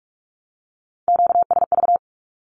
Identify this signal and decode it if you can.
Morse code, heard as a keyed tone, about 44 wpm, 700 Hz. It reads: QSV
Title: Q-Codes Repeated 3x 45wpm